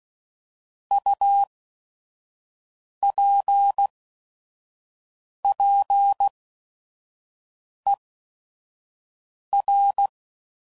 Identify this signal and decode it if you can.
Morse code, heard as a keyed tone, about 16 wpm, spaced as for 5 wpm, 800 Hz. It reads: UPPER
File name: Prefixes / By First Letter / Us